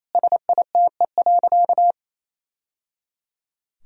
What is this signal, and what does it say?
Morse code, heard as a keyed tone, about 27 wpm, 700 Hz. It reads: SITE.
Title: Sentences from Top 1000 Words - Review Part 4 - 28wpm